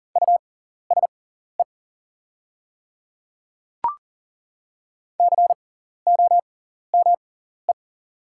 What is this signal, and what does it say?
Morse code, heard as a keyed tone, about 40 wpm, spaced as for 14 wpm, 690 Hz. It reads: USE COME
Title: 6x Character Spacing - Top 100 Words - 40wpm